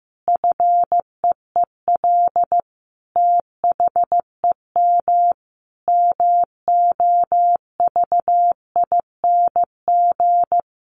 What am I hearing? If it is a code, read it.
Morse code, heard as a keyed tone, about 15 wpm, 700 Hz. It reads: FEEL THEM MOVING
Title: A Sound Of Thunder 15wpm